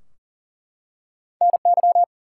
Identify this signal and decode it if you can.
Morse code, heard as a keyed tone, about 38 wpm, 700 Hz. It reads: NY
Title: US States and Canadian Provinces 40wpm